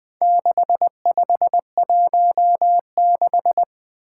Morse code, heard as a keyed tone, about 20 wpm, 700 Hz. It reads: 6516
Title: Four Challenging Letter-Number Combinations 20wpm